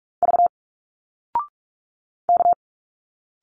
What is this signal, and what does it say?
Morse code, heard as a keyed tone, about 46 wpm, 700 Hz. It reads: V K
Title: Single Letters 45wpm